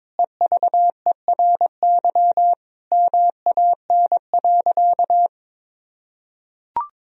Morse code, heard as a keyed tone, about 22 wpm, 700 Hz. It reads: EVERY MAN.
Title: Sentences from Top 1000 Words - Review Part 1 - 22wpm